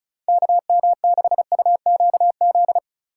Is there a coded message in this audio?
Yes: Morse code from 0.3 to 2.8 s, heard as a keyed tone, about 45 wpm, 700 Hz. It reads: KM6UQZ